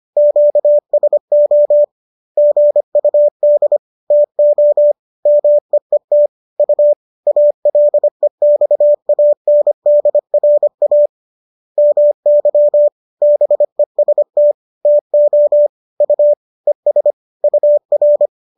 Morse code, heard as a keyed tone, about 25 wpm, 590 Hz. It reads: QSO GUD TO MEET U ALEXANDRA MY BEST TO U ES UR